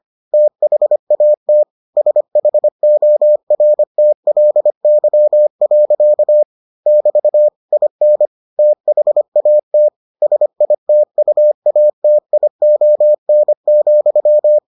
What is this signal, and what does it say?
Morse code, heard as a keyed tone, about 25 wpm, 600 Hz. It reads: THAT SHORTLY. <BT> IN THAT SITUATION,